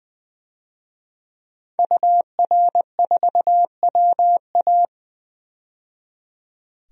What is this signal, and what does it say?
Morse code, heard as a keyed tone, about 20 wpm, 700 Hz. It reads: UR4WA
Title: DXCC-100 Call Signs - 5 Characters - 20wpm